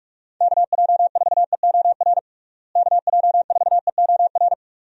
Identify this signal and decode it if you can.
Morse code, heard as a keyed tone, about 48 wpm, 700 Hz. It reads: KJ4EOR KJ4EOR
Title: USA Call Signs - 6 Characters - Repeated 2x - 45wpm